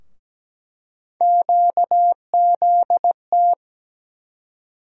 Morse code, heard as a keyed tone, about 17 wpm, 700 Hz. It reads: QZT
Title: USA Call Sign Postfixes - 3 Letters 17wpm